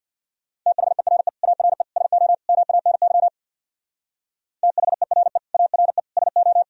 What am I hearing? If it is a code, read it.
Morse code, heard as a keyed tone, about 62 wpm, 700 Hz. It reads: THERE ARE SO MANY THERE ARE SO
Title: Speed-Racing - Sets of 4 Words 50wpm